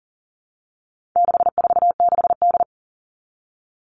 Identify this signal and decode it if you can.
Morse code, heard as a keyed tone, about 40 wpm, 700 Hz. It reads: 646D
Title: All Cut Numbers - 4-Digit 40wpm